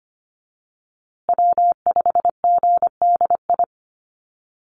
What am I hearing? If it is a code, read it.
Morse code, heard as a keyed tone, about 25 wpm, 700 Hz. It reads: W5GDI
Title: USA Call Signs Repeated 2x 25wpm